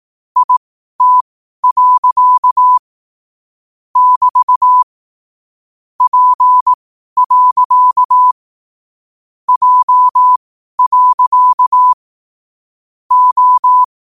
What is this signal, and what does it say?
Morse code, heard as a keyed tone, about 18 wpm, spaced as for 12 wpm, 1000 Hz. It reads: IT. <BT> P. J. O
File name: qotdfiles / 12wpm